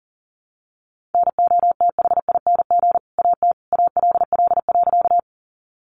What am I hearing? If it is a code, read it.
Morse code, heard as a keyed tone, about 40 wpm, 700 Hz. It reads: NOTHING AT ALL.
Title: Sentences from Top 500 Words 40wpm